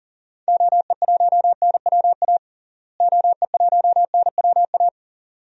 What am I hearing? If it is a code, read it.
Morse code, heard as a keyed tone, about 40 wpm, 700 Hz. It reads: OE1NWA OE1NWA
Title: DXCC-100 Call Signs - 6 Characters - Repeated 2x - 40wpm